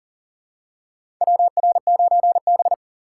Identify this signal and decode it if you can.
Morse code, heard as a keyed tone, about 40 wpm, 700 Hz. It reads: WR9B
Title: USA Call Signs - 2x Character Spacing - 40wpm